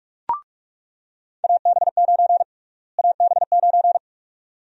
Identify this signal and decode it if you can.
Morse code, heard as a keyed tone, about 44 wpm, 700 Hz. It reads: AB9 AB9
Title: USA Call Sign Prefixes - Repeated 2x - 45wpm